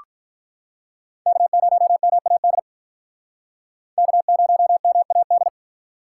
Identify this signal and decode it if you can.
Morse code, heard as a keyed tone, about 45 wpm, 700 Hz. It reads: K0MAD K0MAD